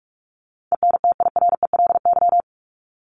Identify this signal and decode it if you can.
Morse code, heard as a keyed tone, about 44 wpm, 700 Hz. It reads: ENTIRELY
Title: LY Suffix - Top 100 Words with Base Words 45wpm